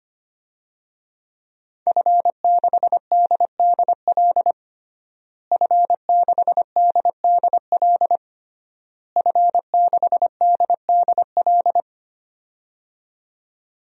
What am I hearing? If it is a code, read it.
Morse code, heard as a keyed tone, about 25 wpm, 700 Hz. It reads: F6DDL F6DDL F6DDL
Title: DXCC-100 Call Signs - 5 Characters - Repeated 3x - 25wpm